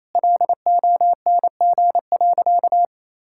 Morse code, heard as a keyed tone, about 28 wpm, 700 Hz. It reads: LONG.
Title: Sentences from Top 1000 Words - Review Part 5 - 28wpm